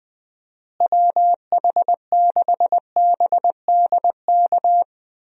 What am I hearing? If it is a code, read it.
Morse code, heard as a keyed tone, about 20 wpm, 700 Hz. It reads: WH6BDK
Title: USA Call Signs - 6 Characters - Repeated 2x - 20wpm